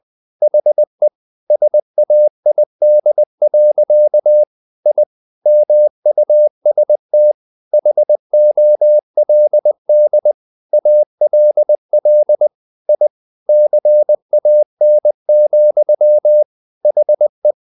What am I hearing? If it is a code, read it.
Morse code, heard as a keyed tone, about 20 wpm, 600 Hz. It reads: HE SAID. I MUST HOLD ALL I CAN, HE
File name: the old man and the sea part 2 20wpm